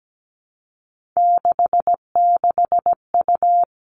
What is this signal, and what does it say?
Morse code, heard as a keyed tone, about 17 wpm, 700 Hz. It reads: 66U
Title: Three Challenging Letter-Number Combinations 17wpm